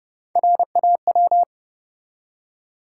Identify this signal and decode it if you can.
Morse code, heard as a keyed tone, about 30 wpm, 700 Hz. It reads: RAW
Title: Three Letter Words - 30wpm